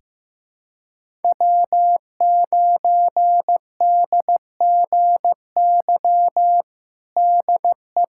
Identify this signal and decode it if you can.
Morse code, heard as a keyed tone, about 15 wpm, 700 Hz. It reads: W9DGY DE
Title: USA Callsign DE Callsign - 5 Characters - 15wpm